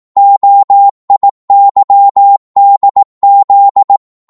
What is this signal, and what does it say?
Morse code, heard as a keyed tone, about 18 wpm, 800 Hz. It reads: OIYDZ